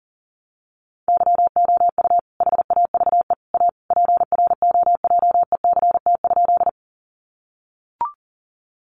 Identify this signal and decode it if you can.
Morse code, heard as a keyed tone, about 40 wpm, 700 Hz. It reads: YOU HAVE A PROJECT?
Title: Sentences from Top 1400 Words 40wpm